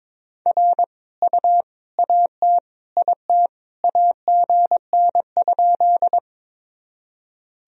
Morse code, heard as a keyed tone, about 22 wpm, 700 Hz. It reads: R U AT IT AGN?
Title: Sentences from Top 200 Words and Abbreviations 22wpm